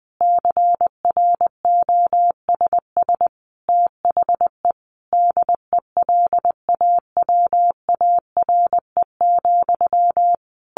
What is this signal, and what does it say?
Morse code, heard as a keyed tone, about 20 wpm, 700 Hz. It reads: CROSS THE DELAWARE,
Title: A Sound Of Thunder 20wpm